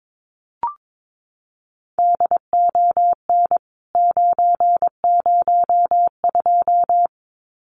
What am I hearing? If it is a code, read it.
Morse code, heard as a keyed tone, about 22 wpm, 700 Hz. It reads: DON 902
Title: CWT Exchange 22wpm